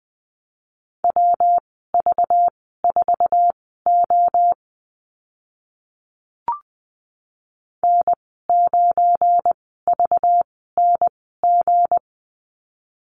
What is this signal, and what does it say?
Morse code, heard as a keyed tone, about 20 wpm, spaced as for 14 wpm, 700 Hz. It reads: WV4O N9VNG
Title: 219 Lesson 35g - Number 0 - Call Signs 20wpm